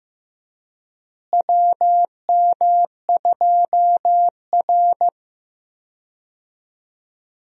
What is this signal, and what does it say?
Morse code, heard as a keyed tone, about 15 wpm, 700 Hz. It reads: WM2R
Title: USA Call Signs - 4 Characters - Repeated 3x - 15wpm